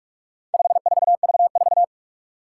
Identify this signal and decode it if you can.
Morse code, heard as a keyed tone, about 41 wpm, 700 Hz. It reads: 54V4
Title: Four Challenging Letter-Number Combinations 45wpm